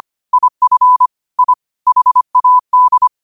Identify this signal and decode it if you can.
Morse code, heard as a keyed tone, about 25 wpm, 1000 Hz. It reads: IF I HAD
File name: qotdfiles / 25wpm